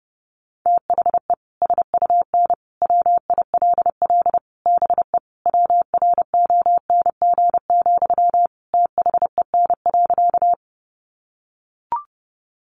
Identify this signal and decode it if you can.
Morse code, heard as a keyed tone, about 30 wpm, 700 Hz. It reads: THE SUN WILL BE WRONG, THEN.
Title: Sentences from Top 900 Words 30wpm